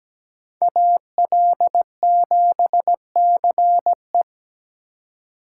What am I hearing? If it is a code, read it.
Morse code, heard as a keyed tone, about 17 wpm, 700 Hz. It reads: AL7CE
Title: USA Call Signs - 5 Characters - 2x Character Spacing - 17wpm